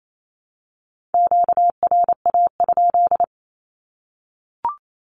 Morse code, heard as a keyed tone, about 28 wpm, 700 Hz. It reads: QRA?
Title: Q-Codes Repeated 3x 28wpm